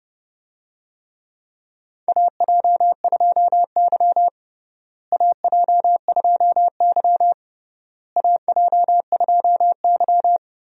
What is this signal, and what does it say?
Morse code, heard as a keyed tone, about 32 wpm, 700 Hz. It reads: AJ2Y AJ2Y AJ2Y
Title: USA Call Signs - 4 Characters - Repeated 3x - 30wpm